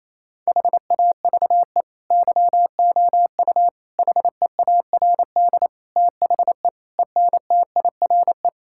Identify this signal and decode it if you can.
Morse code, heard as a keyed tone, about 28 wpm, 700 Hz. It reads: HAVE YOU HEARD THE ENTIRE
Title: Sentences from Top 1000 Words - Review Part 5 - 28wpm